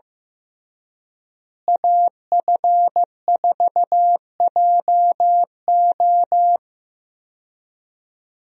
Morse code, heard as a keyed tone, about 15 wpm, 700 Hz. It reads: AF4JO
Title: USA Call Signs - 5 Characters - Repeated 3x - 15wpm